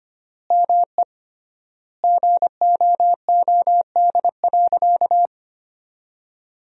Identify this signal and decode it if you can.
Morse code, heard as a keyed tone, about 25 wpm, 700 Hz. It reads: ME GOOD.
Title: 3x Word-Spacing - Sentences from Top 500 Words - 25wpm